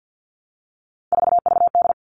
Morse code, heard as a keyed tone, about 50 wpm, 700 Hz. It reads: 4VD
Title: All Cut Numbers - 3-Digit 50wpm